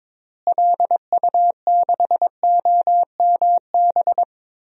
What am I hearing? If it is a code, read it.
Morse code, heard as a keyed tone, about 22 wpm, 700 Hz. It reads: LU6OMB
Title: DXCC-100 Call Signs - 6 Characters - Repeated 3x - 22wpm